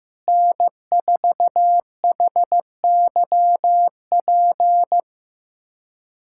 Morse code, heard as a keyed tone, about 15 wpm, 700 Hz. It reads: N4HYP
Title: USA Callsign DE Callsign - 5 Characters - 15wpm